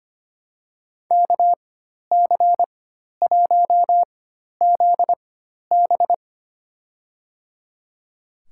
Morse code, heard as a keyed tone, about 25 wpm, spaced as for 12 wpm, 700 Hz. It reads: KC1ZB
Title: USA Call Signs - 5 Characters - 4x Character Spacing - 25wpm